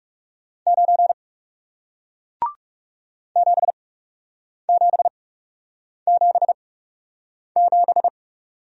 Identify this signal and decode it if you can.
Morse code, heard as a keyed tone, about 39 wpm, spaced as for 8 wpm, 700 Hz. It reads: 9 7777